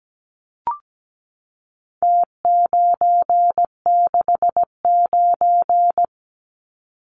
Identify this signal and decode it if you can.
Morse code, heard as a keyed tone, about 17 wpm, 700 Hz. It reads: T969
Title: Numbers - Contest Style - 4-Digit 17wpm